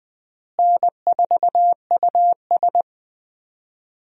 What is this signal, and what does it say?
Morse code, heard as a keyed tone, about 21 wpm, 700 Hz. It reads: N4US
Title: USA Call Signs - 4 Characters - Repeated 3x - 20wpm